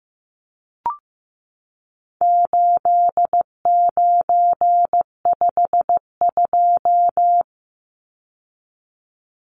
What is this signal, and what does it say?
Morse code, heard as a keyed tone, about 15 wpm, 700 Hz. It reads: 8952
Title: Numbers - 4-Digit 15wpm